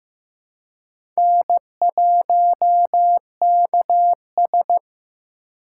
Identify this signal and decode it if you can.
Morse code, heard as a keyed tone, about 15 wpm, 700 Hz. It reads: N1KS
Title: USA Call Signs - 4 Characters - 6x Character Spacing - 15wpm